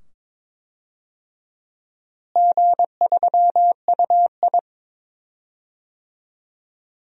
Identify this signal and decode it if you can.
Morse code, heard as a keyed tone, about 22 wpm, 700 Hz. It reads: G3UI
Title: DXCC-100 Call Signs - 4 Characters - Repeated 2x - 22wpm